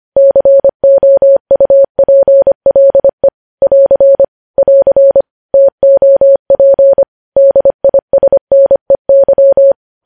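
Morse code, heard as a keyed tone, about 25 wpm, 550 Hz. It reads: COUPLE <AR> <AR> TOP DISNEY
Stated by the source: Morse code news headlines at 25 wpm.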